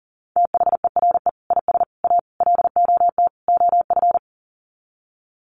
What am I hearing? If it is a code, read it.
Morse code, heard as a keyed tone, about 40 wpm, 700 Hz. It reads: THERE IS A LOT OF
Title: Sets of 5 Words 40wpm